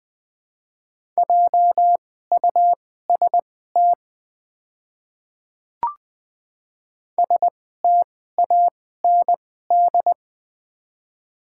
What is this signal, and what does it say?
Morse code, heard as a keyed tone, about 20 wpm, spaced as for 14 wpm, 700 Hz. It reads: JUST STAND
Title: 2x Character Spacing - Top 300 Words - 20wpm